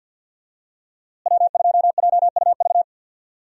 Morse code, heard as a keyed tone, about 47 wpm, 700 Hz. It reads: W2JUV
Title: USA Call Signs - 5 Characters - Repeated 2x - 50wpm